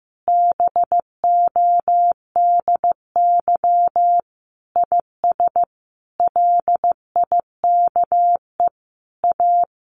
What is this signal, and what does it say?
Morse code, heard as a keyed tone, about 15 wpm, 700 Hz. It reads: BODY IS LIKE A
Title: Sentences from Top 1000 Words 15wpm